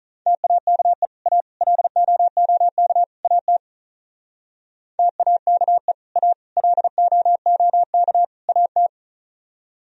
Morse code, heard as a keyed tone, about 46 wpm, 700 Hz. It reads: TAKE A LOOK AT TAKE A LOOK AT